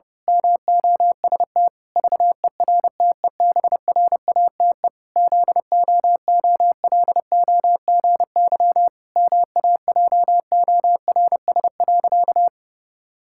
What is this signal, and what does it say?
Morse code, heard as a keyed tone, about 30 wpm, 700 Hz. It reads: MOST VERTEBRATE ZOOLOGY MAJORS.
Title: Pangram Challenge v2 - 30wpm